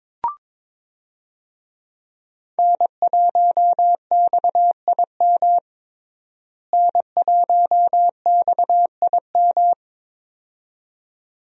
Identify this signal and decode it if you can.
Morse code, heard as a keyed tone, about 22 wpm, 700 Hz. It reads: N1XIM N1XIM
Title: USA Call Signs - 5 Characters - Repeated 2x - 22wpm